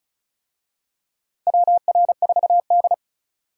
Morse code, heard as a keyed tone, about 36 wpm, 700 Hz. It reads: WR4D
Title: USA Call Signs - 4 Characters - 35wpm